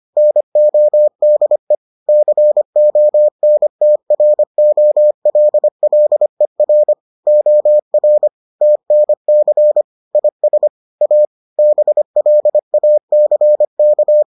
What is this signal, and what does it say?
Morse code, heard as a keyed tone, about 25 wpm, 600 Hz. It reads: NODE CONTROLLER OR TNC IS A BLACK